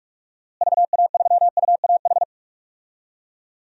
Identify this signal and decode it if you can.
Morse code, heard as a keyed tone, about 41 wpm, 700 Hz. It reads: VA3UAH
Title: DXCC-100 Call Signs - 6 Characters - Repeated 3x - 45wpm